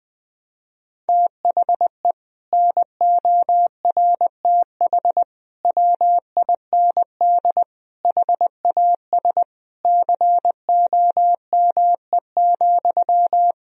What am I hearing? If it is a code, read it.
Morse code, heard as a keyed tone, about 20 wpm, 700 Hz. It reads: THE NORTH WIND HAS COME,
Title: Sentences from Top 1000 Words - Review Part 3 - 20wpm